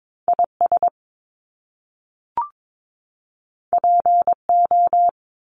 Morse code, heard as a keyed tone, about 22 wpm, 700 Hz. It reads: IS PO
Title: Top Two Letter Combinations in Words 22wpm